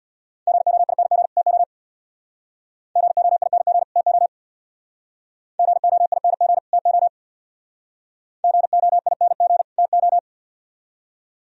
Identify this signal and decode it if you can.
Morse code, heard as a keyed tone, about 65 wpm, 700 Hz. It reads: GOING TO GOING TO GOING TO GOING TO